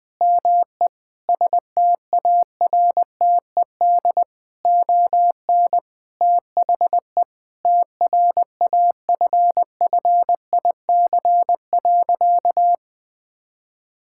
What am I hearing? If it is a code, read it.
Morse code, heard as a keyed tone, about 21 wpm, 700 Hz. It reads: ME STARTED ON THE TRAFFIC.
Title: Sentences with 8-Word Phrases - 20wpm